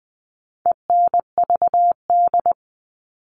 Morse code, heard as a keyed tone, about 20 wpm, 700 Hz. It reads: ENVD
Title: All Cut Numbers - 4-Digit 20wpm